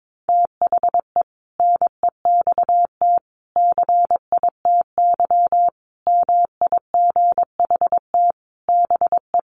Morse code, heard as a keyed tone, about 22 wpm, 700 Hz. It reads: THE NEXT CITY MIGHT BE
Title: Sentences from Top 1400 Words 22wpm